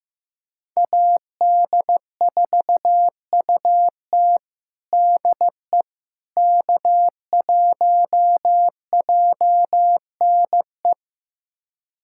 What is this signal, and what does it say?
Morse code, heard as a keyed tone, about 15 wpm, 700 Hz. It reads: AD4UT DE K1JNE
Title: USA Callsign DE Callsign - 5 Characters - 15wpm